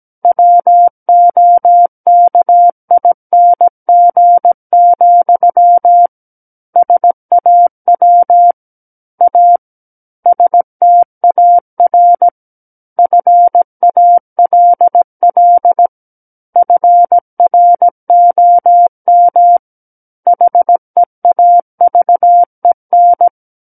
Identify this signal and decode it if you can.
Morse code, heard as a keyed tone, about 17 wpm, 700 Hz. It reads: WOKING, SAW A STAR FALL FROM HEAVEN
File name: Book One / 08-Chapter 8 - 17 WPM